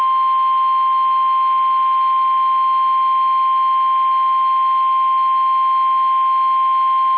Emergency Alert System (SAME) tones (the 1050 Hz attention tone). It is heard from 0.5 to 6.1 s.
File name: eas / KUSO / KUSO